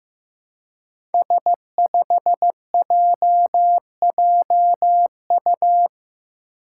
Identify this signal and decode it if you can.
Morse code, heard as a keyed tone, about 15 wpm, 700 Hz. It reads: S5JJU
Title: Five Challenging Letter-Number Combinations 15wpm